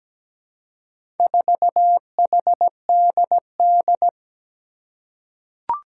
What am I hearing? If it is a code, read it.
Morse code, heard as a keyed tone, about 17 wpm, 700 Hz. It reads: 4HDD
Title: Four Challenging Letter-Number Combinations 17wpm